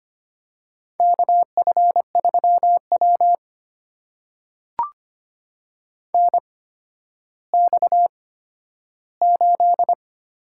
Morse code, heard as a keyed tone, about 25 wpm, spaced as for 12 wpm, 700 Hz. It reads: KF3W NX8
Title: USA Call Signs - 4 Characters - 8x Character Spacing - 25wpm